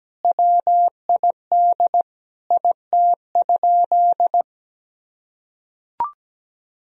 Morse code, heard as a keyed tone, about 17 wpm, 700 Hz. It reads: WID IT?
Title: Sentences from Top 100 Words and Abbreviations 17wpm